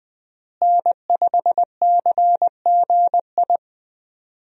Morse code, heard as a keyed tone, about 20 wpm, 700 Hz. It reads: N5CGI
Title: USA Call Signs - 5 Characters - Repeated 2x - 20wpm